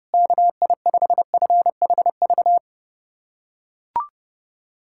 Morse code, heard as a keyed tone, about 32 wpm, 700 Hz. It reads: KI5FHV
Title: USA Callsign DE Callsign - 6 Characters - 30wpm